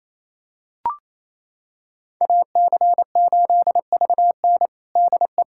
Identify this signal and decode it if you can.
Morse code, heard as a keyed tone, about 27 wpm, 700 Hz. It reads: AC8VN DE
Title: USA Callsign DE Callsign - 5 Characters - 28wpm